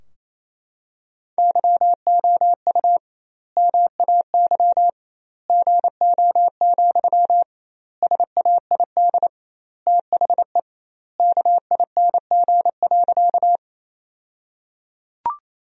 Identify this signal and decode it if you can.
Morse code, heard as a keyed tone, about 28 wpm, 700 Hz. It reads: YOU MAY GO, SAID THE KING.
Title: 2x Word-Spacing - Sentences from Top 400 Words - 28wpm